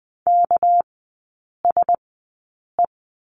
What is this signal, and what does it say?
Morse code, heard as a keyed tone, about 20 wpm, spaced as for 8 wpm, 700 Hz. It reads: KSE
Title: 198 Lesson 32e - Letter k - Three Characters 20wpm